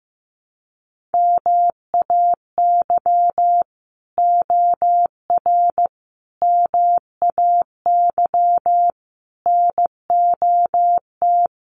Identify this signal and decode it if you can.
Morse code, heard as a keyed tone, about 15 wpm, 700 Hz. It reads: MAY OR MAY NOT
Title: Sets of 5 Words 15wpm